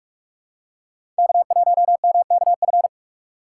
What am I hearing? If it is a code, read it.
Morse code, heard as a keyed tone, about 44 wpm, 690 Hz. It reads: K1MKF
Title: USA Call Signs - 5 Characters - 8x Character Spacing - 45wpm